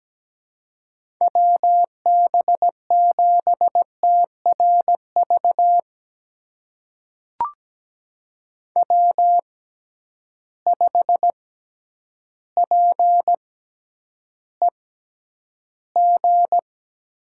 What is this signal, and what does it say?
Morse code, heard as a keyed tone, about 17 wpm, spaced as for 9 wpm, 700 Hz. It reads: WB7TRV W5PEG
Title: USA Call Signs - 6x Character Spacing - 17wpm